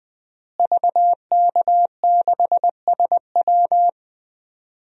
Morse code, heard as a keyed tone, about 20 wpm, 700 Hz. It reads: VK6SW
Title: DXCC-100 Call Signs - 5 Characters - Repeated 2x - 20wpm